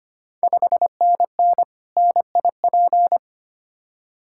Morse code, heard as a keyed tone, about 25 wpm, 700 Hz. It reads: 5NN NIP
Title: Ontario QSO Party Exchange 25wpm